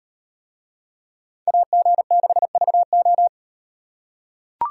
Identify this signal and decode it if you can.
Morse code, heard as a keyed tone, about 38 wpm, 700 Hz. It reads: AG6VO